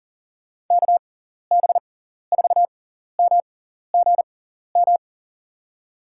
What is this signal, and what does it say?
Morse code, heard as a keyed tone, about 40 wpm, spaced as for 14 wpm, 700 Hz. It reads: KB4MGM
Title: USA Call Signs - 6 Characters - 6x Character Spacing - 40wpm